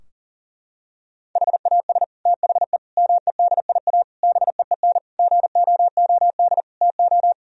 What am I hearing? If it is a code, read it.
Morse code, heard as a keyed tone, about 40 wpm, 700 Hz. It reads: HAS THE MEDIA BEEN GOOD TO
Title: Sentences from Top 1000 Words - Review Part 4 - 40wpm